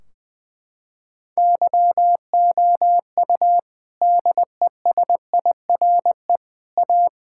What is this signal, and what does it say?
Morse code, heard as a keyed tone, about 21 wpm, 700 Hz. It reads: YOU DESIRE A
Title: Sentences from Top 1600 Words 20wpm